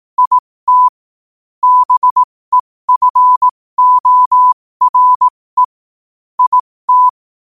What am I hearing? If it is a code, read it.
Morse code, heard as a keyed tone, about 18 wpm, spaced as for 15 wpm, 1000 Hz. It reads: IT BEFORE IT
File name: qotdfiles / 15wpm